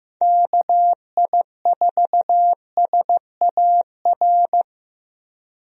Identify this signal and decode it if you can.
Morse code, heard as a keyed tone, about 15 wpm, 700 Hz. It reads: KI4SAR
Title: USA Call Signs - 6 Characters - Repeated 3x - 15wpm